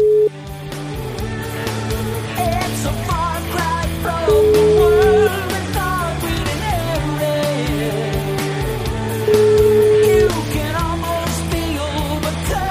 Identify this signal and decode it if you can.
Ringback tone (Europe, 425 Hz), with noise.